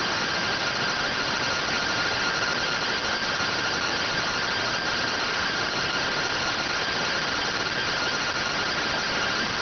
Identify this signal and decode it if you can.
Two-tone teleprinter signal, about 170 Hz shift.